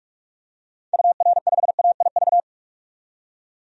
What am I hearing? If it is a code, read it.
Morse code, heard as a keyed tone, about 46 wpm, 700 Hz. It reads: UR5AIV
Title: DXCC-100 Call Signs - 6 Characters - 45wpm